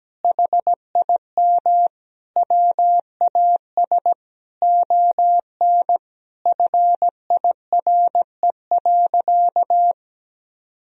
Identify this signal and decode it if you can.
Morse code, heard as a keyed tone, about 17 wpm, 700 Hz. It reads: HIM WAS ON FIRE.
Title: Sentences from Top 1000 Words - Review Part 3 - 17wpm